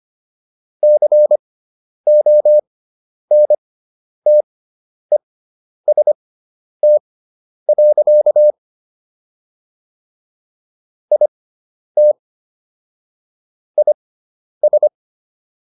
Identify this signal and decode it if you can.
Morse code, heard as a keyed tone, about 25 wpm, spaced as for 10 wpm, 600 Hz. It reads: CONTEST. IT IS